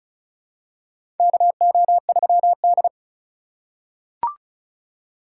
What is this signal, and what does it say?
Morse code, heard as a keyed tone, about 34 wpm, 700 Hz. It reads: KO3D
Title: USA Call Signs - 4 Characters - 3x Character Spacing - 35wpm